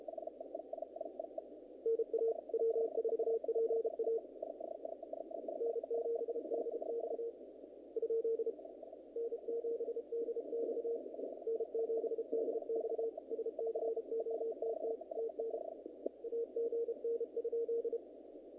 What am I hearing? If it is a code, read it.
Morse code, heard as a keyed tone, about 32 wpm, 460 Hz. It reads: NA W4PA N7DX ? N7DX N7DX SCOTT TN AGN?
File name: contests / naqp-cw-k5tr-op-k5pi- -2000